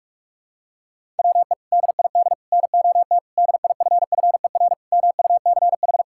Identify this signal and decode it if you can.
Morse code, heard as a keyed tone, about 46 wpm, 700 Hz. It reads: WE DID NOT DIFFER MUCH
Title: Sentences from Top 1000 Words - Review Part 6 - 45wpm